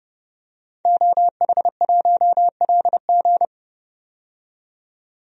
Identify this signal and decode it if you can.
Morse code, heard as a keyed tone, about 32 wpm, 700 Hz. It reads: OH1LG
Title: DXCC-100 Call Signs - 5 Characters - 30wpm